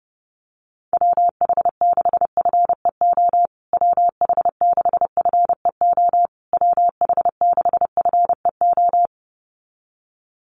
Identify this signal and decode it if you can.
Morse code, heard as a keyed tone, about 30 wpm, 700 Hz. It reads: WH6FEO WH6FEO WH6FEO
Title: USA Call Signs Repeated 3x 30wpm